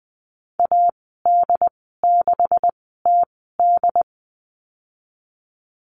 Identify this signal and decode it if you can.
Morse code, heard as a keyed tone, about 20 wpm, spaced as for 15 wpm, 700 Hz. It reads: AD6TD
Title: 186 Lesson 30g - Letter b - Call Signs 20wpm